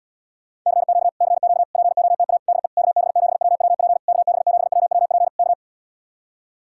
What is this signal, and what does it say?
Morse code, heard as a keyed tone, about 77 wpm, 700 Hz. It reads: CQ CQ POTA DE KB7WWP KB7WWP K